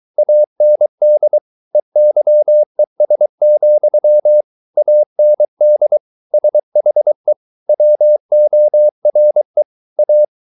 Morse code, heard as a keyed tone, about 23 wpm, 600 Hz. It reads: AND EYES, AND SHE WORE A
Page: Romanen Never af Ken Follet på engelsk.